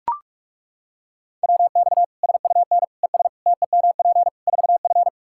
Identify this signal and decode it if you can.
Morse code, heard as a keyed tone, about 41 wpm, 700 Hz. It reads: WX SUN ES TEMP 4F
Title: Practice QSO - Exchange 2 - Info - 45wpm